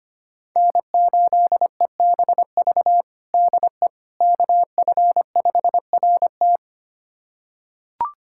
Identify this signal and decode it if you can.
Morse code, heard as a keyed tone, about 25 wpm, 700 Hz. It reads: N8EBV DE KF5RT
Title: USA Callsign DE Callsign - 5 Characters - 25wpm